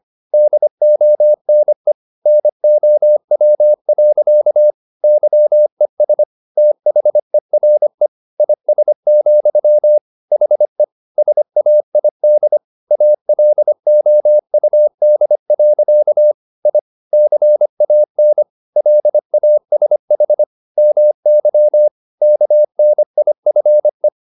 Morse code, heard as a keyed tone, about 25 wpm, 600 Hz. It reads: DONE NOW. YES THERE IS, HE SAID ALOUD. I CAN LASH MY KNIFE